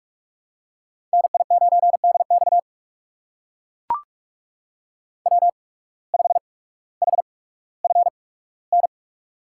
Morse code, heard as a keyed tone, about 43 wpm, spaced as for 19 wpm, 700 Hz. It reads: NI9DX W5HFN
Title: USA Call Signs - 5 Characters - 8x Character Spacing - 45wpm